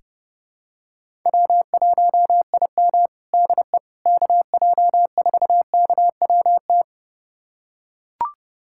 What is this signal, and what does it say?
Morse code, heard as a keyed tone, about 30 wpm, 700 Hz. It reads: W1IM DE KJ4KWT
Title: USA Callsign DE Callsign 30wpm